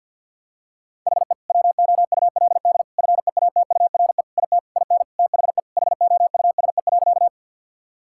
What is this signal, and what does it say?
Morse code, heard as a keyed tone, about 50 wpm, 700 Hz. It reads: HE WOULD FEATURE IT IN THE HOUSE.
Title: Sentences from Top 1000 Words - Review Part 5 - 50wpm